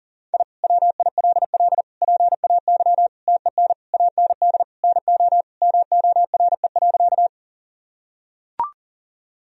Morse code, heard as a keyed tone, about 37 wpm, 700 Hz. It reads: I WILL PAY TEN AND NO MORE.
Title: Sentences from Top 1000 Words - Review Part 2 - 40wpm